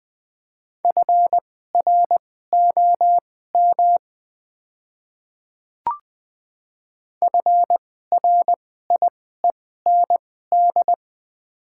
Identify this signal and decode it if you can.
Morse code, heard as a keyed tone, about 21 wpm, spaced as for 15 wpm, 700 Hz. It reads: FROM FRIEND
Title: 2x Character Spacing - Top 300 Words - 20wpm